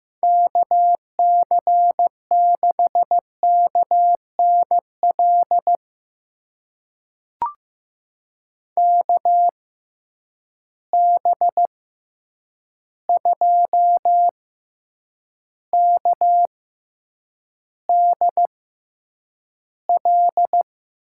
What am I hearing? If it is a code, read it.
Morse code, heard as a keyed tone, about 15 wpm, spaced as for 8 wpm, 700 Hz. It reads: KC6KNL KB2KDL
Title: USA Call Signs - 6 Characters - 6x Character Spacing - 15wpm